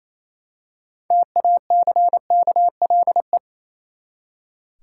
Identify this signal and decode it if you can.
Morse code, heard as a keyed tone, about 27 wpm, 700 Hz. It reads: TACKLE
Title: Six Letter Words - 28wpm